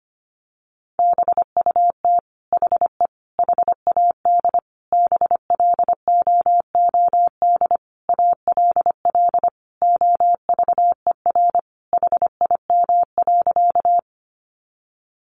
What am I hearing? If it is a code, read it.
Morse code, heard as a keyed tone, about 25 wpm, 700 Hz. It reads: BUT HE HAD BLOOD ALL OVER HIM.
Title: Sentences from Top 1100 Words 25wpm